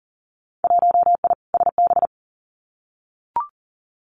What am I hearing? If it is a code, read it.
Morse code, heard as a keyed tone, about 40 wpm, 700 Hz. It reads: 1I SB
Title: Winter Field Day Exchanges 40wpm